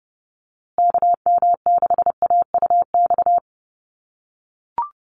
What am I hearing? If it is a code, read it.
Morse code, heard as a keyed tone, about 30 wpm, 700 Hz. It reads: KM6AUX
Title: USA Call Signs Repeated 2x 30wpm